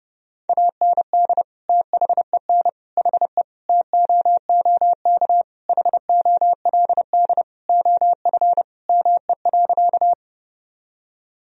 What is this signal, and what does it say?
Morse code, heard as a keyed tone, about 32 wpm, 700 Hz. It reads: AND THEN HE TOOK HOLD OF ME.
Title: Sentences from Top 1000 Words - Review Part 2 - 30wpm